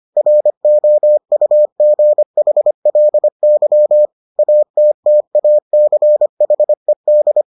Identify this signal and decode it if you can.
Morse code, heard as a keyed tone, about 25 wpm, 600 Hz. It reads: ROUGHLY ATTACHED